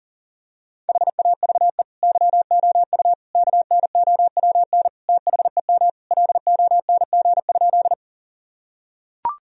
Morse code, heard as a keyed tone, about 40 wpm, 700 Hz. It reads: HAVE YOU KNOWN THEM LONG?
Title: Sentences from Top 1000 Words - Review Part 3 - 40wpm